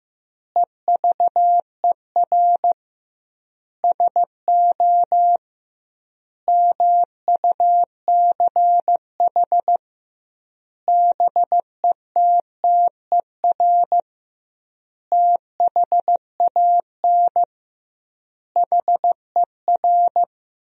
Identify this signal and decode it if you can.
Morse code, heard as a keyed tone, about 15 wpm, 700 Hz. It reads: EVER SO MUCH BETTER THAN HER
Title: 2x Word-Spacing - Sentences from Top 400 Words - 15wpm